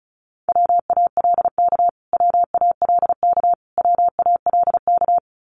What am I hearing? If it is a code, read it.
Morse code, heard as a keyed tone, about 34 wpm, 700 Hz. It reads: WALK WALK WALK
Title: Top 500 Words Repeated 35wpm